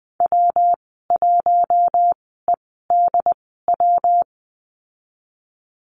Morse code, heard as a keyed tone, about 20 wpm, spaced as for 15 wpm, 700 Hz. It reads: W1EDW
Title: 154 Lesson 25g - Letter p - Call Signs 20wpm